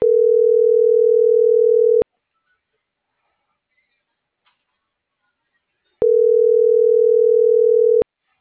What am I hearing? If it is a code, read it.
Ringback tone (North America), 440 + 480 Hz.